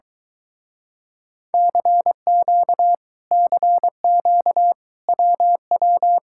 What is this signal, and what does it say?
Morse code, heard as a keyed tone, about 23 wpm, 700 Hz. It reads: CQ CQ WW